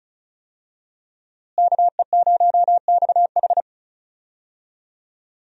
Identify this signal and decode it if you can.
Morse code, heard as a keyed tone, about 35 wpm, 700 Hz. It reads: KE0XH
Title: USA Call Signs - 5 Characters - Repeated 2x - 35wpm